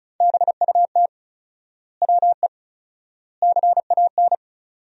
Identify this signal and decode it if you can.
Morse code, heard as a keyed tone, about 36 wpm, 700 Hz. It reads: BUT WE CAN
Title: 4x Word-Spacing - Sentences from Top 100 Words - 35wpm